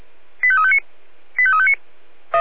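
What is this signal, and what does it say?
Five-tone selective call (ZVEI), heard as quick runs of five tones, tones of about 70 ms. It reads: 85359 85359
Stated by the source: ZVEI five-tone call 85350, received on SDR.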